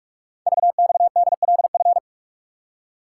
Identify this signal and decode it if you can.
Morse code, heard as a keyed tone, about 44 wpm, 700 Hz. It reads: VXDLF
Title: Code Groups - 5 Letters - 45wpm